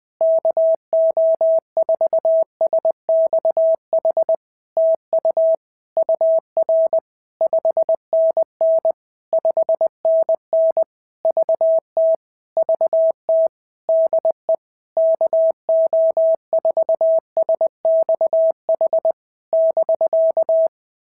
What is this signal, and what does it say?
Morse code, heard as a keyed tone, about 20 wpm, 650 Hz. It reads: KO4SXH TU UR 5NN 5NN VT VT DE KO4SXH <BK>
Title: POTA QSO Practice - 20wpm